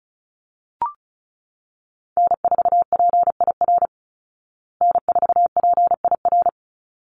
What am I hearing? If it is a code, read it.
Morse code, heard as a keyed tone, about 34 wpm, 700 Hz. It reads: N4PIR N4PIR
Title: Call Signs Repeated 3x - 35wpm